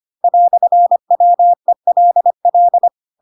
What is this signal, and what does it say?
Morse code, heard as a keyed tone, about 25 wpm, 700 Hz. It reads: "WELL
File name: Pinocchio / Pinocchio ch2 25 25 ik1hge